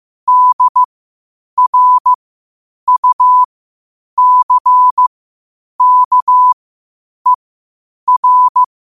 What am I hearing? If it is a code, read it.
Morse code, heard as a keyed tone, about 15 wpm, spaced as for 8 wpm, 1000 Hz. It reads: DRUCKER
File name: qotdfiles / 8wpm